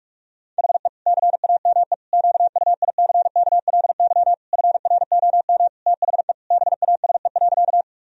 Morse code, heard as a keyed tone, about 41 wpm, 700 Hz. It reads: HE CAME QUICKLY FROM THE BASE.
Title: Sentences from Top 1000 Words - Review Part 1 - 45wpm